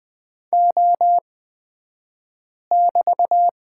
Morse code, heard as a keyed tone, about 20 wpm, spaced as for 5 wpm, 700 Hz. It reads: O<BT>
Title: 223 Lesson 36d - BT - Two Characters 20wpm